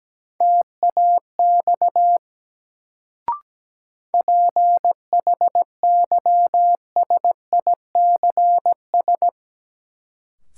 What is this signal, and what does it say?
Morse code, heard as a keyed tone, about 17 wpm, 700 Hz. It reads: TAX PHYSICS
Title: Sentences from Top 1000 Words - Review Part 5 - New Word Drill - 17wpm